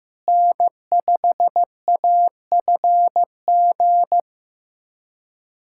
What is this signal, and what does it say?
Morse code, heard as a keyed tone, about 15 wpm, 700 Hz. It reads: N5AFG
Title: USA Callsign DE Callsign - 5 Characters - 15wpm